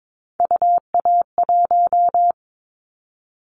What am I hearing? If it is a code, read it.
Morse code, heard as a keyed tone, about 22 wpm, 700 Hz. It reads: UA1
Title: Callsign Prefixes for Top 25 DXCC - Repeated 3x - 22wpm